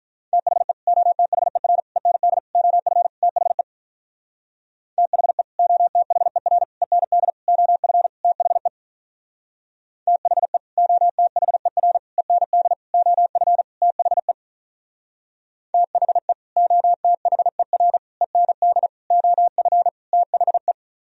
Morse code, heard as a keyed tone, about 45 wpm, 700 Hz. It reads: THE OTHER END OF THE THE OTHER END OF THE THE OTHER END OF THE THE OTHER END OF THE